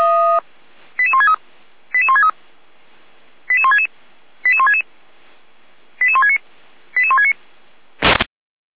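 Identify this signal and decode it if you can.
Five-tone selective call (ZVEI), heard as quick runs of five tones, tones of about 70 ms. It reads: 88163 88163 88166 88166 88169 88169
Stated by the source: ZVEI five-tone call 88402, received by SDR, from feuerwehr-orlamuende (fire brigade Orlamünde).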